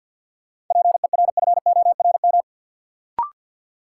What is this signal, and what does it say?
Morse code, heard as a keyed tone, about 53 wpm, 700 Hz. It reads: PERFORM
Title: Sentences from Top 1500 Words - New Word Drill Alt - 50wpm